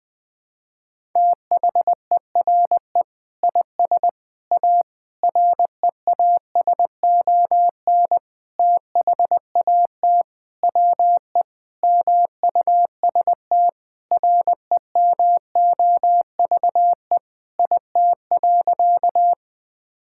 Morse code, heard as a keyed tone, about 20 wpm, 700 Hz. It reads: THERE IS A REASON THAT WE MUST REMOVE IT.
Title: Sentences from Top 1000 Words - Review Part 5 - 20wpm